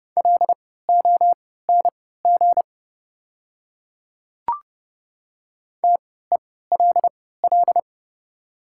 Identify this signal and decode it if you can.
Morse code, heard as a keyed tone, about 31 wpm, spaced as for 17 wpm, 700 Hz. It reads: LONG TELL
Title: 3x Character Spacing - Top 200 Words - 30wpm